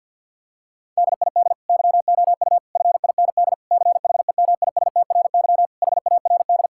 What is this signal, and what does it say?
Morse code, heard as a keyed tone, about 53 wpm, 700 Hz. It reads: DID YOU FIND CHEMISTRY HARD
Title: Sentences from Top 1000 Words - Review Part 6 - 50wpm